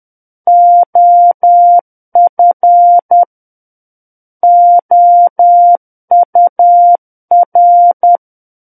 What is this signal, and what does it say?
Morse code, heard as a keyed tone, about 10 wpm, 700 Hz. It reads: OF OUR
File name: Book One / 01-Chapter 1 - 10 WPM